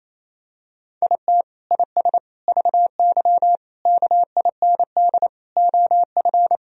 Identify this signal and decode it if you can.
Morse code, heard as a keyed tone, about 28 wpm, 700 Hz. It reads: IT IS VY KIND OF
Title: Sentences from Top 200 Words and Abbreviations 28wpm